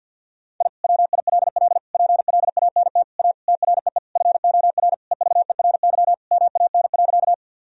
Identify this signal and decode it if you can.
Morse code, heard as a keyed tone, about 50 wpm, 700 Hz. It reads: I WILL PLANT A TREE FOR EVERY MAN.
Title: Sentences from Top 1000 Words - Review Part 1 - 50wpm